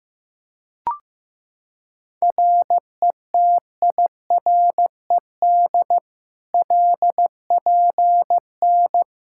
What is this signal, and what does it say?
Morse code, heard as a keyed tone, about 15 wpm, 700 Hz. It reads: RETIRED LPN
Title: Practice QSO - Exchange 3 - More Info - 15wpm